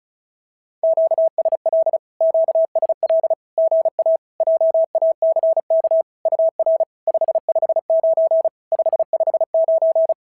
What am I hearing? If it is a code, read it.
Morse code, heard as a keyed tone, about 36 wpm, 650 Hz. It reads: QSL QSL GA JACK UR 559 559
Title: SOTA QSO Practice 35wpm